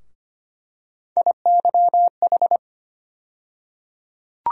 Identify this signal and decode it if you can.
Morse code, heard as a keyed tone, about 25 wpm, 700 Hz. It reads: IYH
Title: USA Call Sign Postfixes - 3 Letters 25wpm